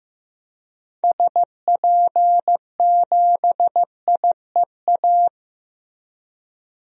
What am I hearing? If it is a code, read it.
Morse code, heard as a keyed tone, about 15 wpm, 700 Hz. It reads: SP7IEA
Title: DXCC-100 Call Signs - 6 Characters - Repeated 3x - 15wpm